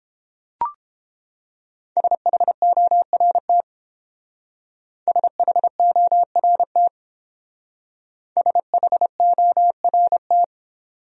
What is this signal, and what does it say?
Morse code, heard as a keyed tone, about 28 wpm, 700 Hz. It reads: SHORT SHORT SHORT